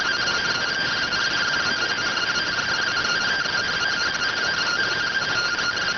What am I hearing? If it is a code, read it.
Two-tone teleprinter signal, about 170 Hz shift.